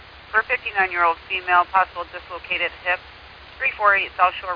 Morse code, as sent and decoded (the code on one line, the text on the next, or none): none